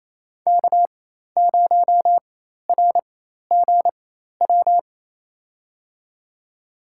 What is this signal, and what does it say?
Morse code, heard as a keyed tone, about 28 wpm, spaced as for 13 wpm, 700 Hz. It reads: K0RGW
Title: USA Call Signs - 5 Characters - 4x Character Spacing - 28wpm